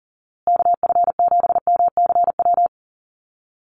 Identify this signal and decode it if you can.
Morse code, heard as a keyed tone, about 38 wpm, 700 Hz. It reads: KF7MCW
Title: USA Call Signs Repeated 2x 40wpm